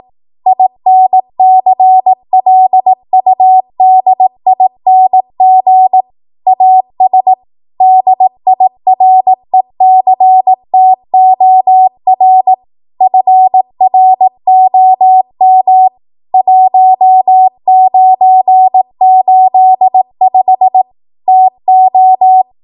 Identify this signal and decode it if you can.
Morse code, heard as a keyed tone, about 18 wpm, 750 Hz. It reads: INCLUDING AS DIRECTOR FROM 1985 TO